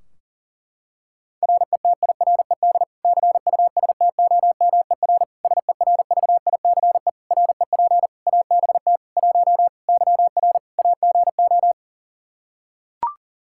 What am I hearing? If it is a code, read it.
Morse code, heard as a keyed tone, about 37 wpm, 700 Hz. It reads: RETIRED CUSTOMER SERVICE REP ABT 1 YR AGO
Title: Practice QSO - Exchange 3 - More Info - 40wpm